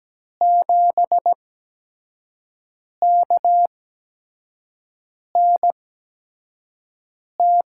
Morse code, heard as a keyed tone, about 17 wpm, spaced as for 5 wpm, 700 Hz. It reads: 7KNT
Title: USA Call Signs - 6 Characters - 8x Character Spacing - 17wpm